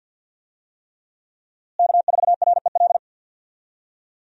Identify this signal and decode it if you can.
Morse code, heard as a keyed tone, about 49 wpm, 700 Hz. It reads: K4REL
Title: USA Call Signs - 5 Characters - Repeated 3x - 50wpm